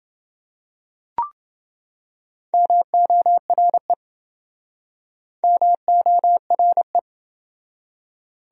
Morse code, heard as a keyed tone, about 29 wpm, 700 Hz. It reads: MORE MORE